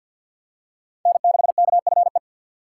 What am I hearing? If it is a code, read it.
Morse code, heard as a keyed tone, about 53 wpm, 700 Hz. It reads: N6KFE
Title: USA Call Signs - 5 Characters - 3x Character Spacing - 50wpm